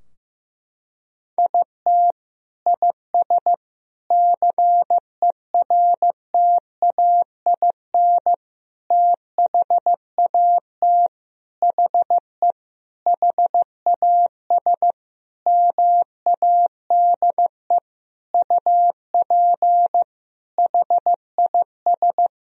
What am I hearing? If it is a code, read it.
Morse code, heard as a keyed tone, about 15 wpm, 700 Hz. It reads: IT IS CERTAIN THAT HE HAS MADE UP HIS
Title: Sentences from Top 1000 Words - Review Part 2 - 15wpm